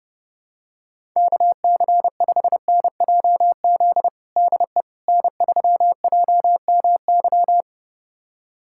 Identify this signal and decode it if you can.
Morse code, heard as a keyed tone, about 31 wpm, 700 Hz. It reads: KC5NJZ DE N3JMY
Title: USA Callsign DE Callsign 30wpm